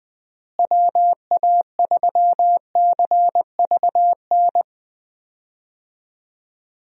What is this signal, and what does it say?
Morse code, heard as a keyed tone, about 21 wpm, 700 Hz. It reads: WA3CVN
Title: USA Call Signs - 6 Characters - Repeated 3x - 20wpm